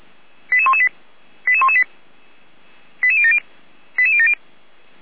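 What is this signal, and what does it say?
Five-tone selective call (ZVEI), heard as quick runs of five tones, tones of about 70 ms. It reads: 88118 88118 88070 88070
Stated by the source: Received with an SDR as ZVEI five-tone selective call, code 88383.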